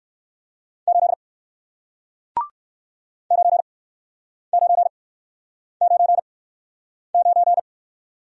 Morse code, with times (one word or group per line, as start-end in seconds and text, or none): 0.9-1.1 8
3.3-7.6 9999